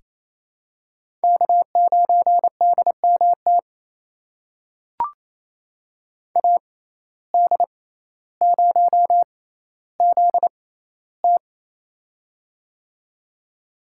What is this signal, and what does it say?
Morse code, heard as a keyed tone, about 28 wpm, spaced as for 14 wpm, 700 Hz. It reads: K9DMT AD0ZT
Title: USA Call Signs - 5 Characters - 6x Character Spacing - 28wpm